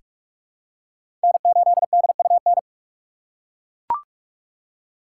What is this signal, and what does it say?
Morse code, heard as a keyed tone, about 42 wpm, 700 Hz. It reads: N8DUN
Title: USA Call Signs - 5 Characters - 4x Character Spacing - 45wpm